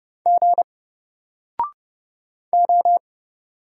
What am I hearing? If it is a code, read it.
Morse code, heard as a keyed tone, about 30 wpm, 700 Hz. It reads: G O
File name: Single-Letter-Number-30wpm